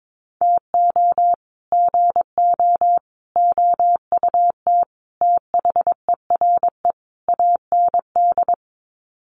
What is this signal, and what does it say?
Morse code, heard as a keyed tone, about 22 wpm, 700 Hz. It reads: TO GO OUT THERE AND
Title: Sets of 5 Words Encore2 22wpm